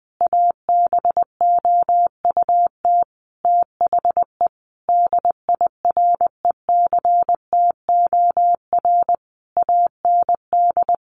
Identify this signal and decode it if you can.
Morse code, heard as a keyed tone, about 20 wpm, 700 Hz. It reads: ABOUT THE DIRECTOR AND
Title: Sentences from Top 1300 Words 20wpm